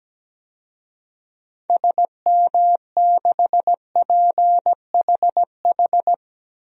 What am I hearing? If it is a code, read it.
Morse code, heard as a keyed tone, about 17 wpm, 700 Hz. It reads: SM6PHH
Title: DXCC-100 Call Signs - 6 Characters - Repeated 2x - 17wpm